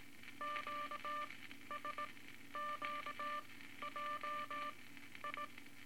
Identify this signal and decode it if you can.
Morse code, heard as a keyed tone, about 18 wpm, spaced as for 12 wpm, 1180 Hz, with noise. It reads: QSQJI
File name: Morse Code Records / 18 A LCM 164 Ten Words Per Minute